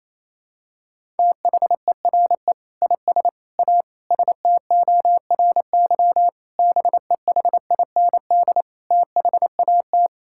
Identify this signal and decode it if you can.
Morse code, heard as a keyed tone, about 27 wpm, 700 Hz. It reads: THERE IS A STORY BEHIND THAT
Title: Sentences from Top 1000 Words - Review Part 5 - 28wpm